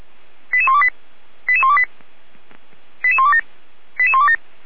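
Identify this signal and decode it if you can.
Five-tone selective call (ZVEI), heard as quick runs of five tones, tones of about 70 ms. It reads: 88128 88128 88127 88127
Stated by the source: ZVEI five-tone selective call 88345 received by SDR.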